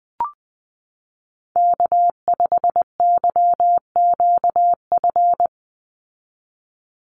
Morse code, heard as a keyed tone, about 20 wpm, 700 Hz. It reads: K5YQF
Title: Call Signs - 20wpm